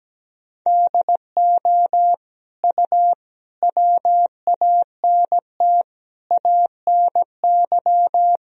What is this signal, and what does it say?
Morse code, heard as a keyed tone, about 17 wpm, 700 Hz. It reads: DO U WANT ANY
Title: Sentences from Top 400 Words and Abbreviations 17wpm